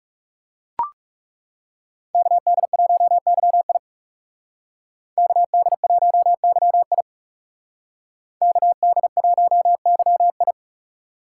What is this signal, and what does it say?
Morse code, heard as a keyed tone, about 36 wpm, 700 Hz. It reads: KD1YI KD1YI KD1YI
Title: Speed Racing - USA Call Signs 30wpm